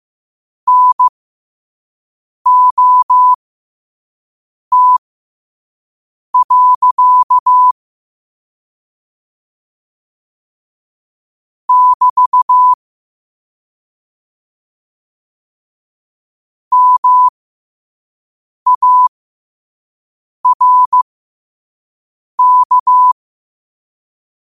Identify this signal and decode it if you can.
Morse code, heard as a keyed tone, about 15 wpm, spaced as for 5 wpm, 1000 Hz. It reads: NOT. <BT> MARK